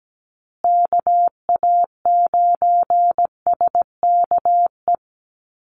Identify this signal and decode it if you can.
Morse code, heard as a keyed tone, about 17 wpm, 700 Hz. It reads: KA9SKE
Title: USA Call Signs Repeated 2x 17wpm